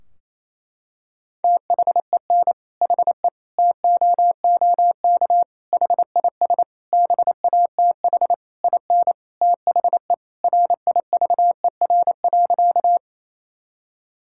Morse code, heard as a keyed tone, about 27 wpm, 700 Hz. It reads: THEN HE TOOK HIS BATH IN THE RIVER.
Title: Sentences from Top 1600 Words 28wpm